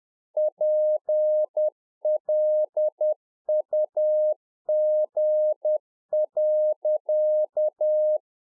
Morse code, heard as a keyed tone, about 10 wpm, 610 Hz. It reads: PLUG.
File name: Czech Army Morse Keys 10WPM